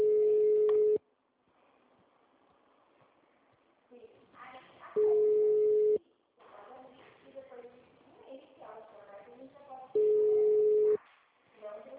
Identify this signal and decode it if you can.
Ringback tone (Europe, 425 Hz), with speech.
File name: out- -2002- - - .13